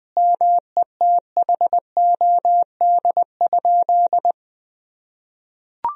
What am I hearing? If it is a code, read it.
Morse code, heard as a keyed tone, about 20 wpm, 700 Hz. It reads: METHOD?
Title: Sentences from Top 1000 Words - Review Part 3 - 20wpm